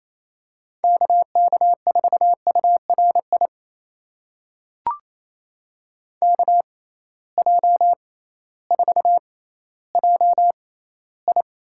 Morse code, heard as a keyed tone, about 28 wpm, spaced as for 15 wpm, 700 Hz. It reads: KK4URI KJ4JI
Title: USA Call Signs - 6x Character Spacing - 28wpm